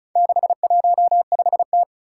Morse code, heard as a keyed tone, about 36 wpm, 700 Hz. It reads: 615T
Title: Four Challenging Letter-Number Combinations 35wpm